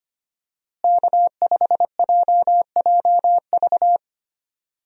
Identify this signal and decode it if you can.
Morse code, heard as a keyed tone, about 25 wpm, 700 Hz. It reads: K5JJV
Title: USA Call Signs - 5 Characters - Repeated 2x - 25wpm